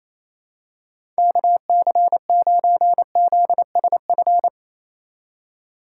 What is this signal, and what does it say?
Morse code, heard as a keyed tone, about 28 wpm, 700 Hz. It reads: KC9ZSF
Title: USA Call Signs - 6 Characters - 3x Character Spacing - 28wpm